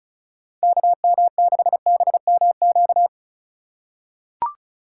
Morse code, heard as a keyed tone, about 36 wpm, 700 Hz. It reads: KM6BMQ
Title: USA Call Signs - 6 Characters - 3x Character Spacing - 35wpm